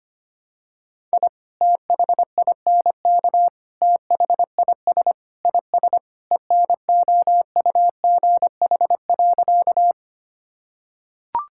Morse code, heard as a keyed tone, about 25 wpm, 700 Hz. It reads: I THINK THIS IS ENOUGH.
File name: practice / 25wpm / Sentences with 4-Word Phrases - 25wpm